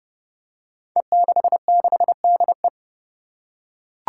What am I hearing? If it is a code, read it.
Morse code, heard as a keyed tone, about 30 wpm, 700 Hz. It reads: E66DE
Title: Five Challenging Letter-Number Combinations 30wpm